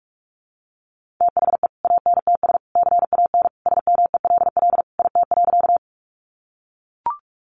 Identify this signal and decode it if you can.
Morse code, heard as a keyed tone, about 44 wpm, 700 Hz. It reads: THE ANTS CAN SMELL IT.
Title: Easy Phrases 45wpm